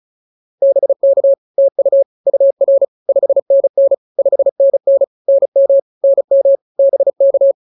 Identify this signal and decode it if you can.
Morse code, heard as a keyed tone, about 34 wpm, 590 Hz. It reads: BK TU UR 5NN 5NN NM NM BK
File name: POTA104 35